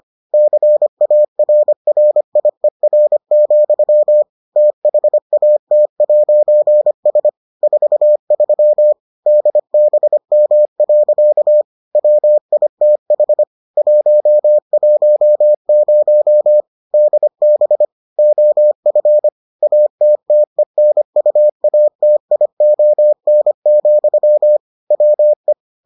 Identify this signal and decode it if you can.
Morse code, heard as a keyed tone, about 25 wpm, 600 Hz. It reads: CARRIER, THAT'S 43 DBM. WITH 110 DB OF ATTENUATION, WE